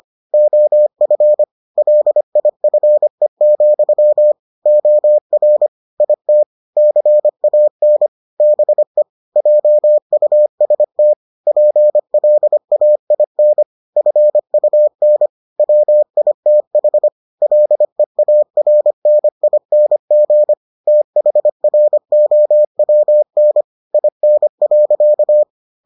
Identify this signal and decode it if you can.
Morse code, heard as a keyed tone, about 25 wpm, 600 Hz. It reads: OF LIFE, OR IT CAN BE JUST PLAIN FUN WITH LEARNING THROWN IN.